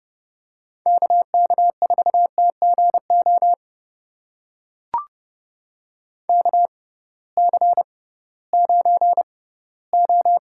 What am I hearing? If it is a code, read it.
Morse code, heard as a keyed tone, about 30 wpm, spaced as for 18 wpm, 700 Hz. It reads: KK4TGO KC9O
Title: USA Call Signs - 6 Characters - 6x Character Spacing - 30wpm